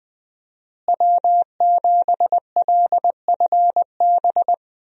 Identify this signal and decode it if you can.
Morse code, heard as a keyed tone, about 20 wpm, 700 Hz. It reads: W7LFB
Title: USA Call Signs - 5 Characters - Repeated 3x - 20wpm